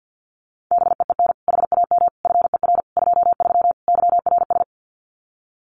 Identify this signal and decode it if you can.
Morse code, heard as a keyed tone, about 50 wpm, 700 Hz. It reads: BEEN HAM FER 23 YRS
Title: Common QSO Phrases Encore 50wpm